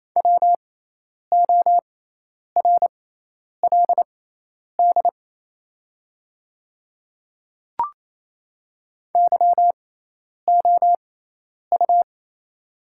Morse code, heard as a keyed tone, about 28 wpm, spaced as for 10 wpm, 700 Hz. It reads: WORLD YOU
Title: 6x Character Spacing - Top 200 Words - 28wpm